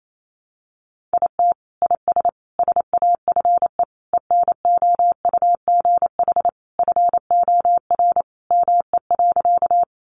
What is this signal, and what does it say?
Morse code, heard as a keyed tone, about 28 wpm, 700 Hz. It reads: IT IS SAFE ENOUGH FOR ME.
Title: Sentences from Top 900 Words 28wpm